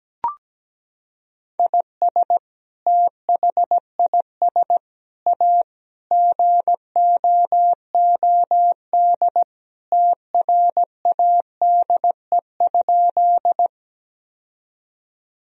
Morse code, heard as a keyed tone, about 17 wpm, 700 Hz. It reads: IS THIS A GOOD TRADE?
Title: Sentences from Top 1000 Words - Review Part 4 - 17wpm